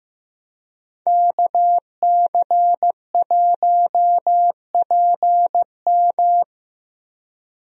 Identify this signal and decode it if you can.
Morse code, heard as a keyed tone, about 15 wpm, 700 Hz. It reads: KC1PM
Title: USA Call Signs - 5 Characters - Repeated 2x - 15wpm